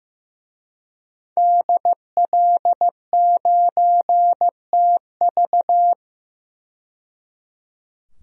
Morse code, heard as a keyed tone, about 15 wpm, 700 Hz. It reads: DL9TV
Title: DXCC-100 Call Signs - 5 Characters - 15wpm